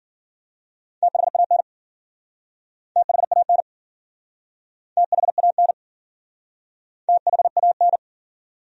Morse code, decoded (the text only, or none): THAN THAN THAN THAN